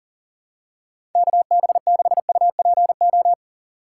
Morse code, heard as a keyed tone, about 40 wpm, 700 Hz. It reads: KB6UPO
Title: USA Call Signs - 6 Characters - 4x Character Spacing - 40wpm